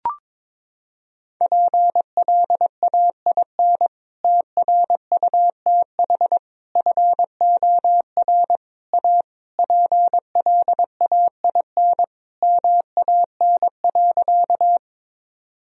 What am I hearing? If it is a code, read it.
Morse code, heard as a keyed tone, about 22 wpm, 700 Hz. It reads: PLAIN TRUTH FOR A PLAIN MAN.
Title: Sentences from Top 1000 Words - Review Part 2 - 22wpm